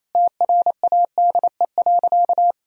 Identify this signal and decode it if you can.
Morse code, heard as a keyed tone, about 28 wpm, 700 Hz. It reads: TRADE.
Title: Sentences from Top 1000 Words - Review Part 4 - 28wpm